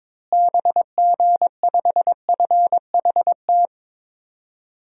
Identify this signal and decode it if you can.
Morse code, heard as a keyed tone, about 21 wpm, 700 Hz. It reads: BG5FHT
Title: DXCC-100 Call Signs - 6 Characters - Repeated 3x - 22wpm